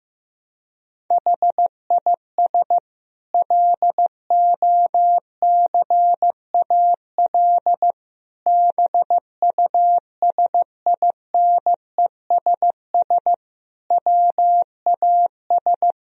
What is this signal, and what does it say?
Morse code, heard as a keyed tone, about 14 wpm, 700 Hz. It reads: HIS LOCAL BUSINESS WAS
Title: Sentences from Top 1000 Words - Review Part 3 - 15wpm